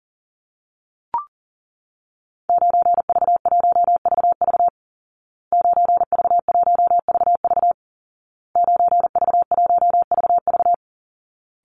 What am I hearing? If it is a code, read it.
Morse code, heard as a keyed tone, about 40 wpm, 700 Hz. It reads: 9V1VV 9V1VV 9V1VV
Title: Call Signs Repeated 2x - 40wpm